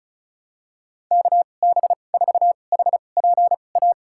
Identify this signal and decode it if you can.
Morse code, heard as a keyed tone, about 36 wpm, spaced as for 25 wpm, 700 Hz. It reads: KB4HPA
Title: USA Call Signs - 6 Characters - 2x Character Spacing - 35wpm